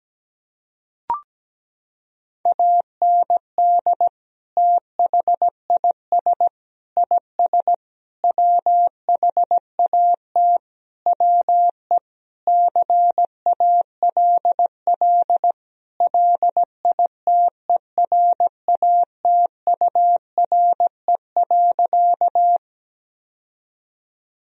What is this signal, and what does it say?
Morse code, heard as a keyed tone, about 17 wpm, 700 Hz. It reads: AND THIS IS WHAT WE CALL LITERATURE.
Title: Sentences from Top 1000 Words - Review Part 3 - 17wpm